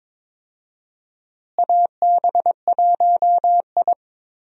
Morse code, heard as a keyed tone, about 22 wpm, 700 Hz. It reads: AB1I
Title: USA Call Signs - 4 Characters - Repeated 3x - 22wpm